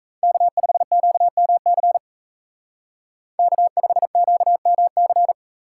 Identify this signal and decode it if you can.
Morse code, heard as a keyed tone, about 33 wpm, 700 Hz. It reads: K5QMC K5QMC
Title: Speed Racing - USA 1x3 Call Signs 28wpm